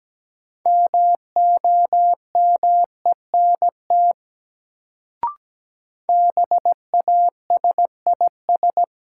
Morse code, heard as a keyed tone, about 17 wpm, 700 Hz. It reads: MOMENT BASIS
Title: Sets of 3 Words - New Word Drill - 17wpm